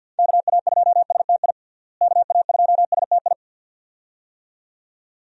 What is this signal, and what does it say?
Morse code, heard as a keyed tone, about 49 wpm, 700 Hz. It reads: KA2STI KA2STI
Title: USA Call Signs - 6 Characters - Repeated 2x - 50wpm